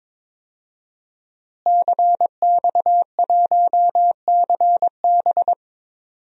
Morse code, heard as a keyed tone, about 21 wpm, 700 Hz. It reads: CX1CB
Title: DXCC-100 Call Signs - 5 Characters - Repeated 3x - 22wpm